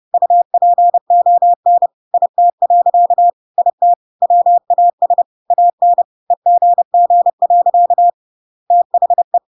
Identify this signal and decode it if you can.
Morse code, heard as a keyed tone, about 30 wpm, 700 Hz. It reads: UPON IT. IT WAS AN EGG. THE
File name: Pinocchio / Pinocchio ch5 30 30 ik1hge